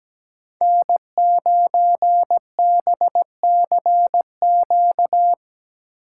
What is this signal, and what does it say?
Morse code, heard as a keyed tone, about 17 wpm, 700 Hz. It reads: N9BCQ
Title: USA Call Signs - 5 Characters - 2x Character Spacing - 17wpm